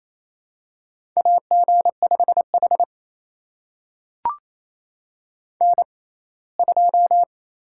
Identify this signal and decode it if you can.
Morse code, heard as a keyed tone, about 28 wpm, spaced as for 19 wpm, 700 Hz. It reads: AG5H N2
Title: USA Call Signs - 6x Character Spacing - 28wpm